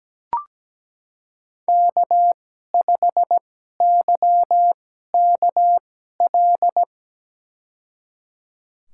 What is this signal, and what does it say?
Morse code, heard as a keyed tone, about 17 wpm, spaced as for 12 wpm, 700 Hz. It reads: K5YKL
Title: USA Call Signs - 5 Characters - 2x Character Spacing - 17wpm